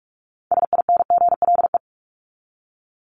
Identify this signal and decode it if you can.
Morse code, heard as a keyed tone, about 44 wpm, 700 Hz. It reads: SINGLE
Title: Top 501-750 Words 45wpm